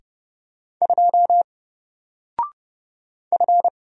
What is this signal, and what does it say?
Morse code, heard as a keyed tone, about 31 wpm, 700 Hz. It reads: 2 F
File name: Single-Letter-Number-30wpm